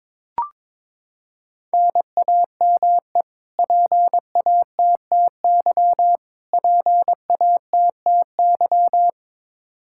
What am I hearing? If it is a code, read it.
Morse code, heard as a keyed tone, about 22 wpm, 700 Hz. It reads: NAME PATTY PATTY
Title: Practice QSO - Exchange 1 - Beginning - 22wpm